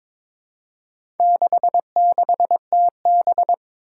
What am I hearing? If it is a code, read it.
Morse code, heard as a keyed tone, about 22 wpm, 700 Hz. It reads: 66TB
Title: Four Challenging Letter-Number Combinations 22wpm